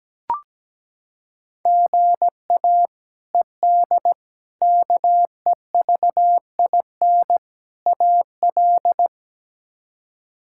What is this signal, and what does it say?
Morse code, heard as a keyed tone, about 17 wpm, 700 Hz. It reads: GA ED KEVIN AL
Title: K1USN SST QSO Practice - 17wpm